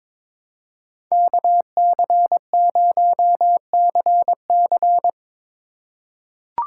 Morse code, heard as a keyed tone, about 22 wpm, 700 Hz. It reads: KC0CC
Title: USA Call Signs - 5 Characters - 2x Character Spacing - 22wpm